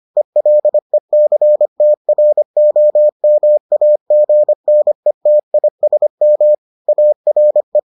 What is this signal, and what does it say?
Morse code, heard as a keyed tone, about 25 wpm, 600 Hz. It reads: ELECTROMAGNETISM ARE